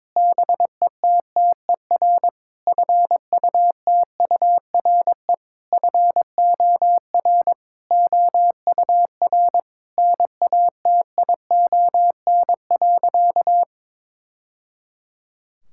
Morse code, heard as a keyed tone, about 22 wpm, 700 Hz. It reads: BETTER FUTURE FOR OUR NATION.
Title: Sentences from Top 1000 Words - Review Part 6 - 22wpm